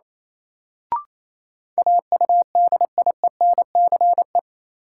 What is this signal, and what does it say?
Morse code, heard as a keyed tone, about 28 wpm, 700 Hz. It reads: AUDIENCE
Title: Sentences from Top 800 Words - New Word Drill - 28wpm